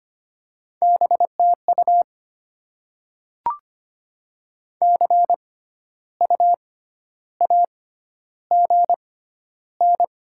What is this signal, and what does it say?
Morse code, heard as a keyed tone, about 25 wpm, spaced as for 12 wpm, 690 Hz. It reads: BTU CU AGN
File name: Top 20 QSO Elements - 6x Character Spacing - 25wpm